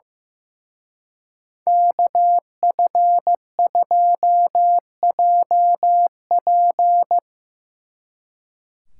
Morse code, heard as a keyed tone, about 17 wpm, 700 Hz. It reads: KF2JP
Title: Speed Racing - USA 2x2 Call Signs 15wpm